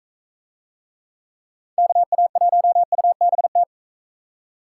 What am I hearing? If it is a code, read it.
Morse code, heard as a keyed tone, about 40 wpm, 700 Hz. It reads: KA1UBT